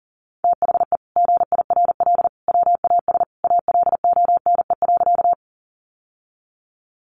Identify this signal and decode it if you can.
Morse code, heard as a keyed tone, about 38 wpm, 700 Hz. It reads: THE GIRL WAS ALONE.
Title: Sentences from Top 1000 Words 40wpm